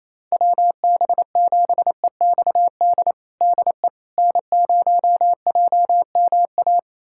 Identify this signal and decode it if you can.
Morse code, heard as a keyed tone, about 28 wpm, 700 Hz. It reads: WB7EXD DE N0JMA
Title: USA Callsign DE Callsign 28wpm